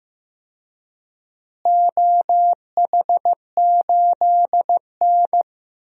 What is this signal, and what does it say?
Morse code, heard as a keyed tone, about 15 wpm, 700 Hz. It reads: OH8N
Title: DXCC-100 Call Signs - 4 Characters - Repeated 3x - 15wpm